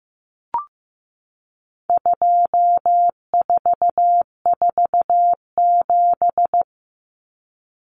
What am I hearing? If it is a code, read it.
Morse code, heard as a keyed tone, about 15 wpm, 700 Hz. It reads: 2447
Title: Numbers - 4-Digit 15wpm